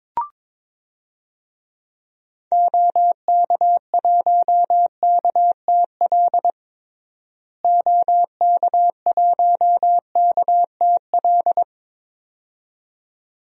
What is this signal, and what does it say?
Morse code, heard as a keyed tone, about 22 wpm, 700 Hz. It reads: OK1KTL OK1KTL
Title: DXCC-100 Call Signs - 6 Characters - Repeated 2x - 22wpm